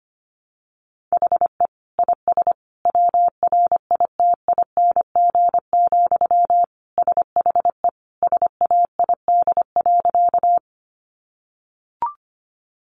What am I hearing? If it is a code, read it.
Morse code, heard as a keyed tone, about 25 wpm, 700 Hz. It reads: HE IS WRITING, SHE SAID.
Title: Sentences from Top 800 Words 25wpm